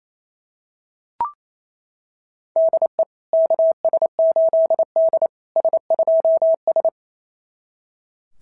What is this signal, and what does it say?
Morse code, heard as a keyed tone, about 27 wpm, 650 Hz. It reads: DE KS8D S2S
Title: SOTA QSO Practice 28wpm